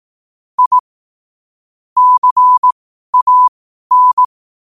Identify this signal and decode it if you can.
Morse code, heard as a keyed tone, about 18 wpm, spaced as for 12 wpm, 1000 Hz. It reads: I CAN
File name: qotdfiles / 12wpm